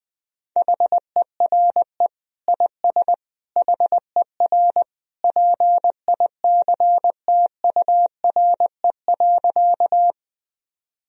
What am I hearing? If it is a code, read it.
Morse code, heard as a keyed tone, about 20 wpm, 700 Hz. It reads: HERE IS HER PICTURE.
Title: Sentences from Top 1000 Words - Review Part 1 - 20wpm